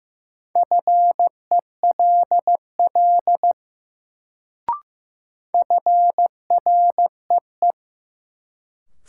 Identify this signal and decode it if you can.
Morse code, heard as a keyed tone, about 15 wpm, 700 Hz. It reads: FELL FREE
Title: Sets of 5 Words Encore - New Word Drill - 15wpm